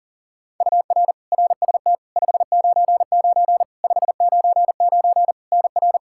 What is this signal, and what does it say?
Morse code, heard as a keyed tone, about 37 wpm, 700 Hz. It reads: UR RST 599 599 NR
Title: Practice QSO - Exchange 1 - Beginning - 40wpm